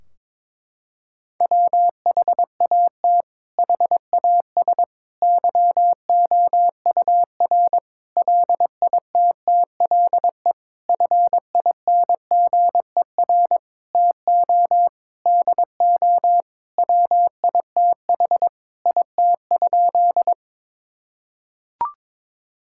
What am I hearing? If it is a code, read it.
Morse code, heard as a keyed tone, about 22 wpm, 700 Hz. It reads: WHAT HAS YOUR LITTLE FINGER TO DO WITH IT?
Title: Sentences from Top 1600 Words 22wpm